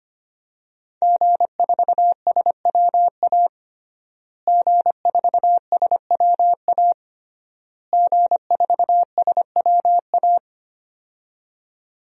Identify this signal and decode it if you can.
Morse code, heard as a keyed tone, about 25 wpm, 700 Hz. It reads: G4SWA G4SWA G4SWA
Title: DXCC-100 Call Signs - 5 Characters - Repeated 3x - 25wpm